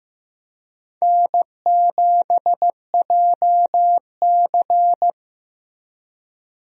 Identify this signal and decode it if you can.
Morse code, heard as a keyed tone, about 16 wpm, 700 Hz. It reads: N7JC
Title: USA Call Signs - 4 Characters - Repeated 2x - 15wpm